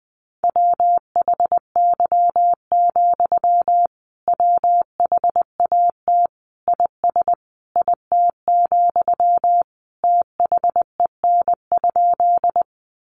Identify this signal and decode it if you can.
Morse code, heard as a keyed tone, about 20 wpm, 700 Hz. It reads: WHY, WHAT IS IT, THEN?
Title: Sentences from Top 200 Words 20wpm